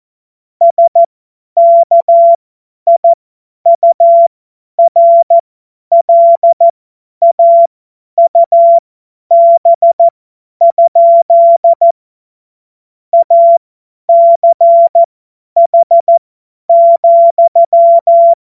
Morse code, heard as a keyed tone, about 14 wpm, spaced as for 10 wpm, 680 Hz. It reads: SKIURLAUB? ACH,
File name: Morsekurs / Witze / Witz 21. März 2023 50 F70